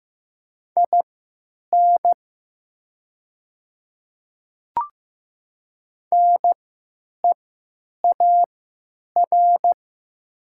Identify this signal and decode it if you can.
Morse code, heard as a keyed tone, about 15 wpm, spaced as for 9 wpm, 700 Hz. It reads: IN NEAR
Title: 3x Character Spacing - Top 300 Words - 15wpm